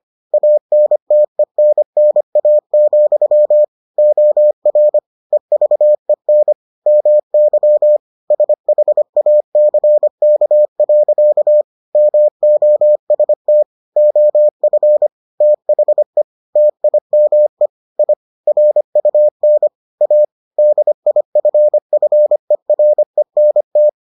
Morse code, heard as a keyed tone, about 25 wpm, 600 Hz. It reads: ANTENNA, OR EVEN MY SHACK. MOST OF THE TIME I RUN A DIFFERENT